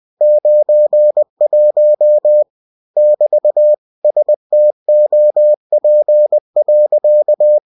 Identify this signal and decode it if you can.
Morse code, heard as a keyed tone, about 20 wpm, 600 Hz. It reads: 91 <BT> STOP.
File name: cw-books / winnie-the-pooh / winnie-20wpm-page-0091